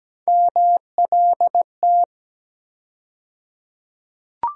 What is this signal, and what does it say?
Morse code, heard as a keyed tone, about 17 wpm, 700 Hz. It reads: MLT
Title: USA Call Sign Postfixes - 3 Letters 17wpm